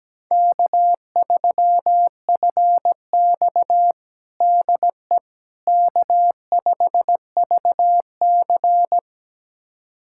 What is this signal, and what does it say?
Morse code, heard as a keyed tone, about 17 wpm, 700 Hz. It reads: K3FX DE K5VC
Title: USA Callsign DE Callsign - 4 Characters - 17wpm